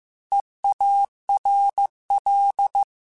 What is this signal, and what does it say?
Morse code, heard as a keyed tone, about 15 wpm, 800 Hz. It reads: EARL
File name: Alphabetical / Alphabetical mp3s / Es @13